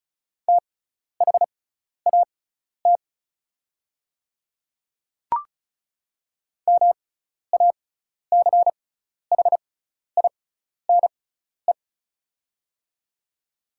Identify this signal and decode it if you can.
Morse code, heard as a keyed tone, about 34 wpm, spaced as for 12 wpm, 690 Hz. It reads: THAT MACHINE
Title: 6x Character Spacing - Top 500 Words - 35wpm